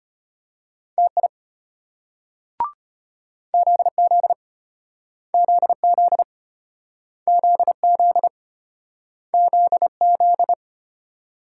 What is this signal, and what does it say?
Morse code, heard as a keyed tone, about 34 wpm, 700 Hz. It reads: TI ZZ ZZ ZZ ZZ